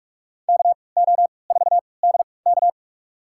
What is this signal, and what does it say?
Morse code, heard as a keyed tone, about 45 wpm, spaced as for 25 wpm, 700 Hz. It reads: KO4DK
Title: USA Call Signs - 3x Character Spacing - 45wpm